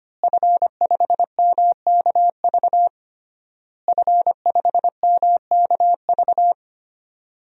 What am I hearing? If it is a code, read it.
Morse code, heard as a keyed tone, about 25 wpm, 700 Hz. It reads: F5MKV F5MKV
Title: DXCC-100 Call Signs - 5 Characters - Repeated 3x - 25wpm